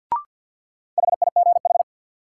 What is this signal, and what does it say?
Morse code, heard as a keyed tone, about 53 wpm, 700 Hz. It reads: HIGH
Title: Sentences from Top 200 Words - New Word Drill - 50wpm